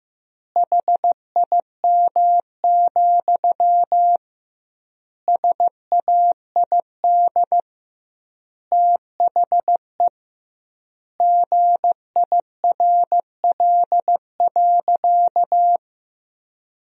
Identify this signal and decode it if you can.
Morse code, heard as a keyed tone, about 16 wpm, 700 Hz. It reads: HIM, SAID THE GIRL.
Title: 2x Word-Spacing - Sentences from Top 400 Words - 15wpm